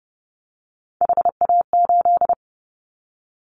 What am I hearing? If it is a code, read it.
Morse code, heard as a keyed tone, about 30 wpm, 700 Hz. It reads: HA8
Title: Callsign Prefixes for Top 25 DXCC - 30wpm